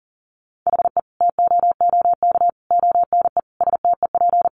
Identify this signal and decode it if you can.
Morse code, heard as a keyed tone, about 40 wpm, 700 Hz. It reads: HE TOOK ONE STEP
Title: Sentences from Top 1300 Words 40wpm